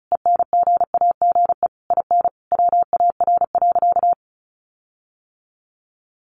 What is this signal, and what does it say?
Morse code, heard as a keyed tone, about 34 wpm, 700 Hz. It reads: ENGAGE IN WAR.
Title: Sentences from Top 1500 Words 35wpm